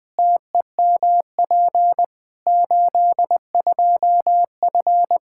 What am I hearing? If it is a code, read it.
Morse code, heard as a keyed tone, about 21 wpm, 700 Hz. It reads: TEMP 82F
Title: K1USN SST QSO Practice 20wpm